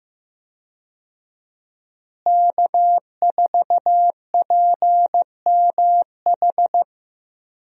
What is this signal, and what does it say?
Morse code, heard as a keyed tone, about 15 wpm, 700 Hz. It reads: K4PMH
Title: USA Call Signs - 5 Characters - Repeated 3x - 15wpm